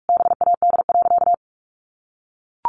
Morse code, heard as a keyed tone, about 44 wpm, 700 Hz. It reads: BAD.
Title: Sentences from Top 900 Words 45wpm